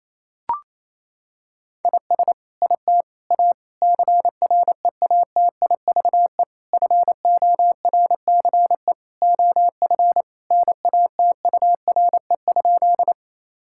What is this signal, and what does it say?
Morse code, heard as a keyed tone, about 28 wpm, 700 Hz. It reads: IS IT A CREATIVE FORCE OF NATURE?
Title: Sentences from Top 1000 Words - Review Part 5 - 28wpm